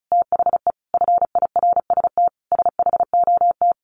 Morse code, heard as a keyed tone, about 34 wpm, 700 Hz. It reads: THE FIRST SHOT
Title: Sentences from Top 1100 Words 35wpm